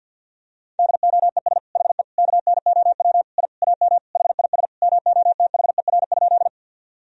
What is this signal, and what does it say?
Morse code, heard as a keyed tone, about 52 wpm, 700 Hz. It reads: DOES HE KNOW I AM HIS MOTHER?
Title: Sentences from Top 1000 Words - Review Part 2 - 50wpm